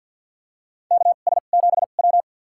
Morse code, heard as a keyed tone, about 53 wpm, spaced as for 37 wpm, 700 Hz. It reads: KS7W
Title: USA Call Signs - 4 Characters - 2x Character Spacing - 50wpm